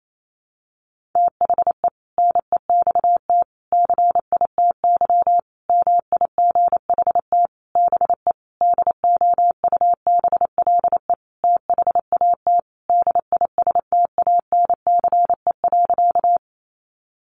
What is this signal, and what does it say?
Morse code, heard as a keyed tone, about 28 wpm, 700 Hz. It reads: THE NEXT CITY MIGHT BE DOUBLE THAT DISTANCE.
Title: Sentences from Top 1400 Words 28wpm